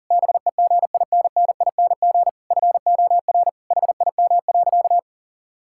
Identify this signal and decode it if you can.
Morse code, heard as a keyed tone, about 40 wpm, 700 Hz. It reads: BEGINNING FOR HIM.
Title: Sentences from Top 1000 Words - Review Part 5 - 40wpm